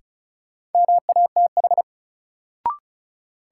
Morse code, heard as a keyed tone, about 36 wpm, 700 Hz. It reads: MATH
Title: Sentences from Top 1000 Words - Review Part 6 - New Word Drill - 35wpm